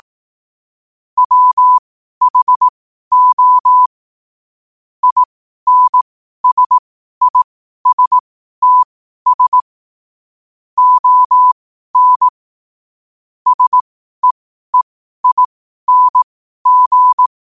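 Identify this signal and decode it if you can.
Morse code, heard as a keyed tone, about 18 wpm, spaced as for 12 wpm, 1000 Hz. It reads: WHO INSISTS ON SEEING
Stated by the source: Morse code quote practice at 12 words per minute.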